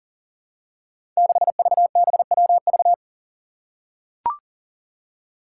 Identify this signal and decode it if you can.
Morse code, heard as a keyed tone, about 40 wpm, 700 Hz. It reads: 6VBWV
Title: Five Challenging Letter-Number Combinations 40wpm